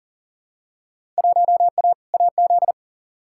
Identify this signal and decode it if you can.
Morse code, heard as a keyed tone, about 40 wpm, 700 Hz. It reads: 1A AZ
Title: Field Day Exchanges - 40wpm